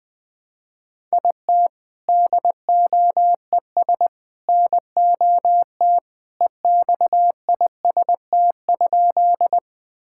Morse code, heard as a keyed tone, about 21 wpm, 700 Hz. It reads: IT DOES NOT EXIST?
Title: Sentences from Top 1000 Words - Review Part 5 - 20wpm